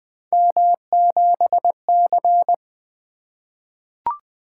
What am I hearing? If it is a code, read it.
Morse code, heard as a keyed tone, about 20 wpm, 700 Hz. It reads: M7C
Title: Three Letter-Number Combinations - 20wpm